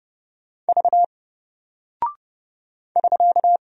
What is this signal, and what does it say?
Morse code, heard as a keyed tone, about 30 wpm, 700 Hz. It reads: V <SK>
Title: Single Character (Letter:Number:Punctuation:Prosign) Alt 30wpm